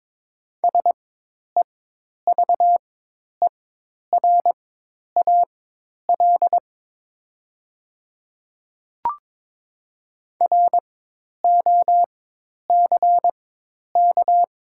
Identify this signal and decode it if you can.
Morse code, heard as a keyed tone, about 22 wpm, spaced as for 10 wpm, 700 Hz. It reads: SEVERAL ROCK
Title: 4x Character Spacing - Top 500 Words - 22wpm